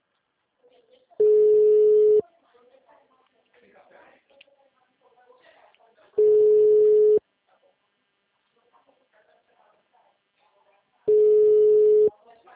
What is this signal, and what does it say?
Ringback tone (Europe, 425 Hz), with speech.
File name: out- -2005- - - .93269